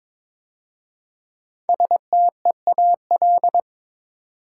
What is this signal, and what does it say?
Morse code, heard as a keyed tone, about 22 wpm, 700 Hz. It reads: STEAL
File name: 5 Letter Words - First 3 Letters Given - 22wpm